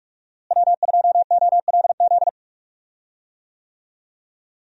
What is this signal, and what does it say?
Morse code, heard as a keyed tone, about 46 wpm, 700 Hz. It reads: W2OLZ
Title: USA Call Signs - 5 Characters - Repeated 3x - 45wpm